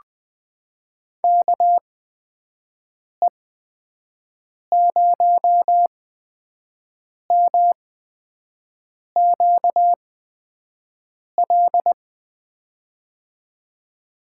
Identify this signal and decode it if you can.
Morse code, heard as a keyed tone, about 20 wpm, spaced as for 5 wpm, 700 Hz. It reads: KE0MQL
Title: USA Call Signs - 8x Character Spacing - 20wpm